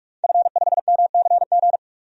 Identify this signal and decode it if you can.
Morse code, heard as a keyed tone, about 46 wpm, 700 Hz. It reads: F5MCG
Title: DXCC-100 Call Signs - 5 Characters - 45wpm